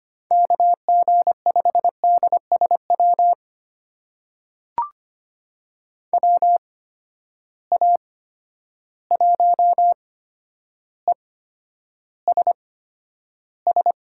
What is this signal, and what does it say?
Morse code, heard as a keyed tone, about 25 wpm, spaced as for 11 wpm, 700 Hz. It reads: KG5DSW WA1ESS
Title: USA Call Signs - 6 Characters - 8x Character Spacing - 25wpm